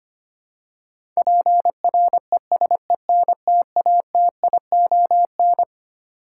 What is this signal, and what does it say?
Morse code, heard as a keyed tone, about 25 wpm, 700 Hz. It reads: PRESENTATION
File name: Twelve Letter Words - 25wpm